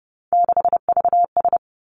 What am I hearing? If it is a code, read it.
Morse code, heard as a keyed tone, about 32 wpm, 700 Hz. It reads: 6VS
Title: Three Challenging Letter-Number Combinations 30wpm